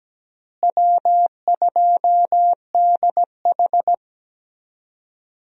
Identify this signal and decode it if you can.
Morse code, heard as a keyed tone, about 17 wpm, 700 Hz. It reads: W2DH
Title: USA Call Signs - 4 Characters - Repeated 3x - 17wpm